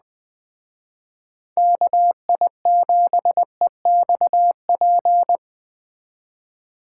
Morse code, heard as a keyed tone, about 20 wpm, 700 Hz. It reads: KI7EXP
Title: Speed Racing - USA 2x3 Call Signs 17wpm